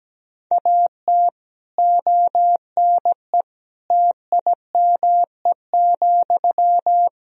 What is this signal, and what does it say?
Morse code, heard as a keyed tone, about 17 wpm, 700 Hz. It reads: AT ONE TIME,
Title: Sentences from Top 1000 Words - Review Part 1 - 17wpm